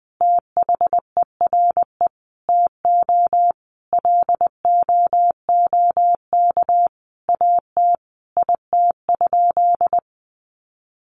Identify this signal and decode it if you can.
Morse code, heard as a keyed tone, about 20 wpm, 700 Hz. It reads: THERE TO LOOK AT IT?
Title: Sentences from Top 100 Words 20wpm